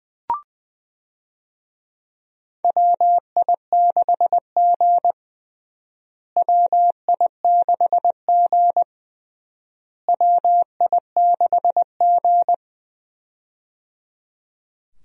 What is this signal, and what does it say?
Morse code, heard as a keyed tone, about 21 wpm, 700 Hz. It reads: WI6G WI6G WI6G
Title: USA Call Signs - 4 Characters - Repeated 3x - 20wpm